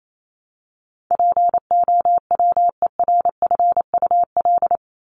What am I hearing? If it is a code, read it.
Morse code, heard as a keyed tone, about 28 wpm, 700 Hz. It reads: POWERFUL
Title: Top 1251-1500 Words 28wpm